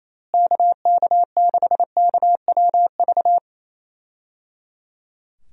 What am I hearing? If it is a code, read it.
Morse code, heard as a keyed tone, about 28 wpm, 700 Hz. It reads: KK6KWV
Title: USA Call Signs - 6 Characters - Repeated 3x - 28wpm